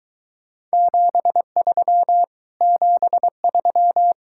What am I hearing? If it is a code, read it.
Morse code, heard as a keyed tone, about 22 wpm, 700 Hz. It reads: 73 73